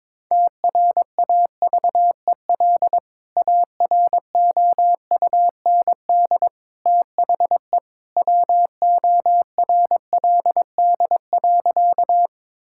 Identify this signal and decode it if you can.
Morse code, heard as a keyed tone, about 22 wpm, 700 Hz. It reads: TRAVEL AROUND THE WORLD.
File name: practice / 22wpm / Sentences with 3-Word Phrases - 22wpm